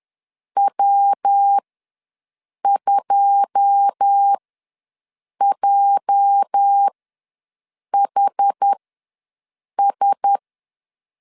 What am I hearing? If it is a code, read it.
Morse code, heard as a keyed tone, about 10 wpm, spaced as for 6 wpm, 800 Hz. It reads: W2JHS
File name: LEZIONI / 9 28 ws 10wpm 4124